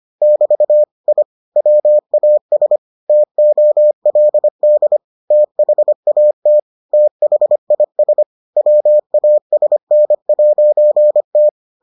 Morse code, heard as a keyed tone, about 25 wpm, 600 Hz. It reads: <BT> I WAS TOLD THAT THIS WASN'T